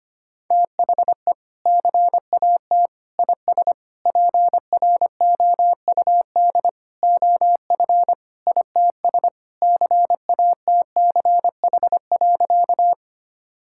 Morse code, heard as a keyed tone, about 25 wpm, 700 Hz. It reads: THE CAT IS PROUD OF ITS CATCH.
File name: practice / 25wpm / Sentences with 2-Word Phrases - 25wpm